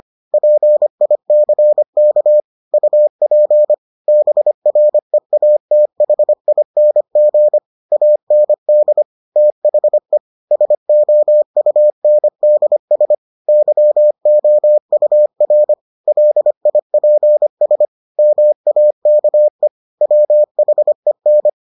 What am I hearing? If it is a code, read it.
Morse code, heard as a keyed tone, about 25 wpm, 600 Hz. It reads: PICK UP BREATHING AND THE SOUNDS YOUR LIPS MAKE WHEN